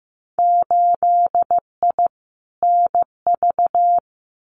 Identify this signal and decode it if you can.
Morse code, heard as a keyed tone, about 15 wpm, 700 Hz. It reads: 8I NV
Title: Winter Field Day Exchanges Encore 15wpm